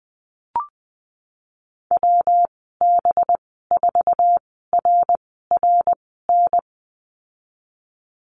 Morse code, heard as a keyed tone, about 20 wpm, spaced as for 15 wpm, 700 Hz. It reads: WB4RRN
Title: 193 Lesson 31g - Letter v - Call Signs 20wpm